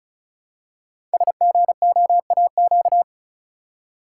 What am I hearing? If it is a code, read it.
Morse code, heard as a keyed tone, about 36 wpm, 700 Hz. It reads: SGOAQ
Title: Code Groups - 5 Letters - 35wpm